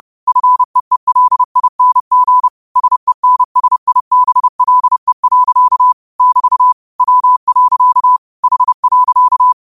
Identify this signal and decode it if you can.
Morse code, heard as a keyed tone, about 30 wpm, 1000 Hz. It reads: FEELING SENSIBLE. <BT> W. H.